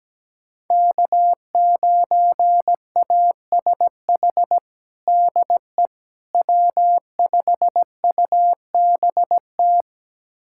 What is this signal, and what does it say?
Morse code, heard as a keyed tone, about 17 wpm, 700 Hz. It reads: K9ASH DE W5UBT
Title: USA Callsign DE Callsign - 5 Characters - 17wpm